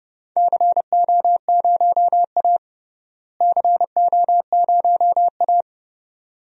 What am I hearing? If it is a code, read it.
Morse code, heard as a keyed tone, about 30 wpm, 700 Hz. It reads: CO0A CO0A
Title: DXCC-100 Call Signs - 4 Characters - Repeated 2x - 30wpm